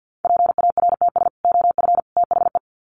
Morse code, heard as a keyed tone, about 50 wpm, 700 Hz. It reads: PARTS OF THE
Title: Sets of 3 Words 50wpm